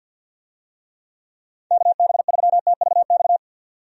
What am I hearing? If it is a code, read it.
Morse code, heard as a keyed tone, about 48 wpm, 700 Hz. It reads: KB3TVX
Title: USA Call Signs - 6 Characters - 50wpm